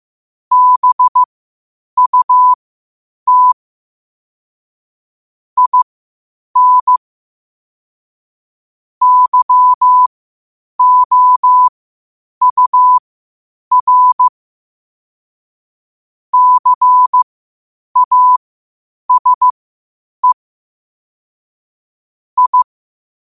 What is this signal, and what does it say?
Morse code, heard as a keyed tone, about 15 wpm, spaced as for 8 wpm, 1000 Hz. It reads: BUT IN YOUR CASE I
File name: qotdfiles / 8wpm